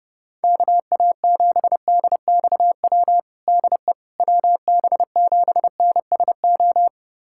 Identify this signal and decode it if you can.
Morse code, heard as a keyed tone, about 32 wpm, 700 Hz. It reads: KA7DXW DE WB7NSO
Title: USA Callsign DE Callsign 30wpm